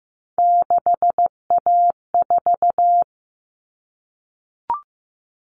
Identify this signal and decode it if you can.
Morse code, heard as a keyed tone, about 15 wpm, 700 Hz. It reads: 6A4
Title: Common Cut Numbers - 3-Digit 15wpm